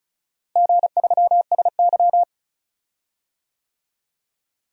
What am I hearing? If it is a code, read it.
Morse code, heard as a keyed tone, about 36 wpm, 700 Hz. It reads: G3SY
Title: DXCC-100 Call Signs - 4 Characters - Repeated 3x - 35wpm